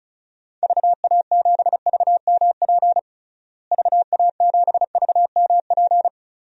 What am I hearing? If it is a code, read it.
Morse code, heard as a keyed tone, about 36 wpm, 700 Hz. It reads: VA7VMP VA7VMP
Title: DXCC-100 Call Signs - 6 Characters - Repeated 2x - 35wpm